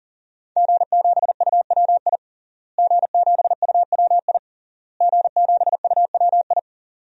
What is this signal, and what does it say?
Morse code, heard as a keyed tone, about 40 wpm, 700 Hz. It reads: G7UWI G7UWI G7UWI
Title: DXCC-100 Call Signs - 5 Characters - Repeated 3x - 40wpm